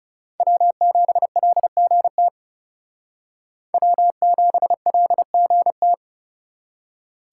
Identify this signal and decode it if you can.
Morse code, heard as a keyed tone, about 37 wpm, 700 Hz. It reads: W7LGT W7LGT
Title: Speed Racing - USA 1x3 Call Signs 30wpm